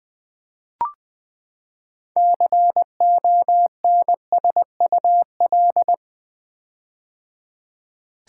Morse code, heard as a keyed tone, about 20 wpm, 700 Hz. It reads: CONSUL
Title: Build A Word - 6 Letters At A Time - 12-Letter Words - 20wpm